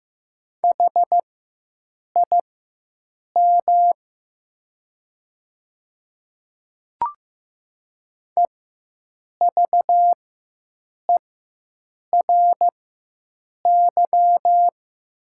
Morse code, heard as a keyed tone, about 15 wpm, spaced as for 7 wpm, 700 Hz. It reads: HIM EVERY
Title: 4x Character Spacing - Top 200 Words - 15wpm